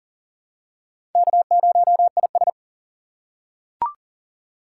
Morse code, heard as a keyed tone, about 37 wpm, 700 Hz. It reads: K0IS
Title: USA Call Signs - 4 Characters - 6x Character Spacing - 40wpm